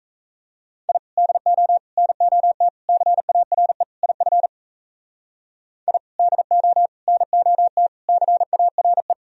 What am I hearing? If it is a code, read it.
Morse code, heard as a keyed tone, about 37 wpm, 700 Hz. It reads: I DO NOT CARE IF I DO NOT CARE